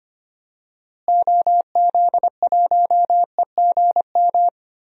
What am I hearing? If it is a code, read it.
Morse code, heard as a keyed tone, about 25 wpm, 700 Hz. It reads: OZ1EGM
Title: DXCC-100 Call Signs - 6 Characters - Repeated 3x - 25wpm